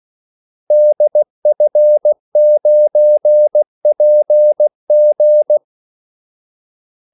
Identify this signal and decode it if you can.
Morse code, heard as a keyed tone, about 16 wpm, 600 Hz. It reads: DF9PG
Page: Kaldesignaler 16 wpm